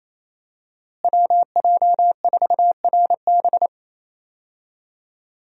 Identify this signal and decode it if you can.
Morse code, heard as a keyed tone, about 28 wpm, 700 Hz. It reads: WJ4RB
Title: USA Call Signs - 5 Characters - Repeated 3x - 28wpm